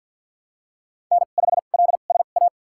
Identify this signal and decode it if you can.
Morse code, heard as a keyed tone, about 52 wpm, spaced as for 37 wpm, 700 Hz. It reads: N5LSA
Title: USA Call Signs - 2x Character Spacing - 50wpm